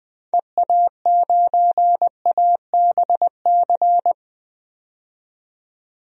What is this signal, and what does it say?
Morse code, heard as a keyed tone, about 20 wpm, 700 Hz. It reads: EA9ABC
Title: DXCC-100 Call Signs - 6 Characters - 20wpm